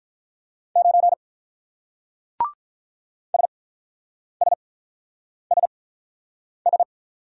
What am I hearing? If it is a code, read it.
Morse code, heard as a keyed tone, about 47 wpm, spaced as for 9 wpm, 700 Hz. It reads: 9 SSSS